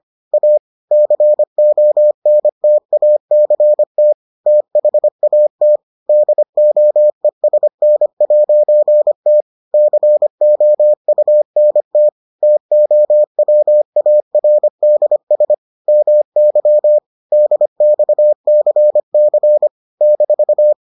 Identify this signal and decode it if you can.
Morse code, heard as a keyed tone, about 25 wpm, 600 Hz. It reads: A CONTACT THAT DOESN'T COUNT TOWARDS MY DXCC -